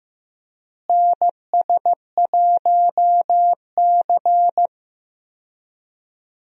Morse code, heard as a keyed tone, about 15 wpm, 700 Hz. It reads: NS1C
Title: USA Call Signs - 4 Characters - Repeated 3x - 15wpm